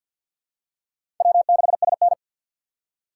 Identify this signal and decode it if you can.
Morse code, heard as a keyed tone, about 50 wpm, 700 Hz. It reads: W6SN
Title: USA Call Signs - 4 Characters - 3x Character Spacing - 50wpm